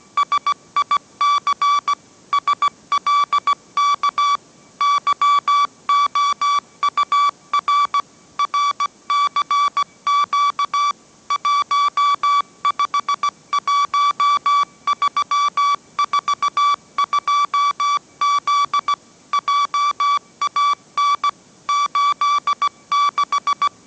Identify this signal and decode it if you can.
Morse code, heard as a keyed tone, about 17 wpm, 1150 Hz. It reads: SIC SLK YOUR RCQ 151342Z JAN 86
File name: mrx / mrx080 18wpm